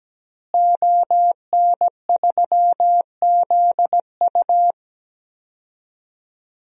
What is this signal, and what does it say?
Morse code, heard as a keyed tone, about 17 wpm, 700 Hz. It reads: ON3ZU
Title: DXCC-100 Call Signs - 5 Characters - Repeated 3x - 17wpm